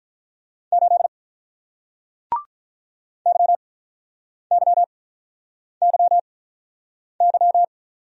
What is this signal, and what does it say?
Morse code, heard as a keyed tone, about 48 wpm, spaced as for 9 wpm, 700 Hz. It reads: 8 YYYY